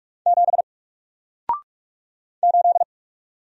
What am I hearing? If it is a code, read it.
Morse code, heard as a keyed tone, about 44 wpm, 700 Hz. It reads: 7 8
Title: Single Character (Letter:Number:Punctuation:Prosign) Alt 45wpm